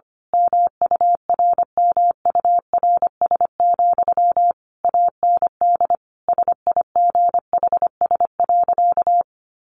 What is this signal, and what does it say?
Morse code, heard as a keyed tone, about 25 wpm, 700 Hz. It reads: MURMURS, AND SIGHS.
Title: A Sound Of Thunder 25wpm